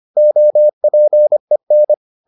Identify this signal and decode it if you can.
Morse code, heard as a keyed tone, about 25 wpm, 600 Hz. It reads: OPEN